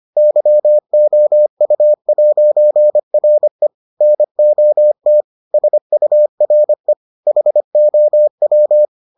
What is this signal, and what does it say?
Morse code, heard as a keyed tone, about 25 wpm, 600 Hz. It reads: YOU'RE NOT SURE HOW